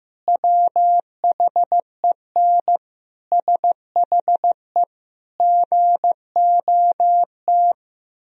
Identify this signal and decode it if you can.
Morse code, heard as a keyed tone, about 15 wpm, 700 Hz. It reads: WHEN SHE GOT
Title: Sentences from Top 1000 Words - Review Part 1 - 15wpm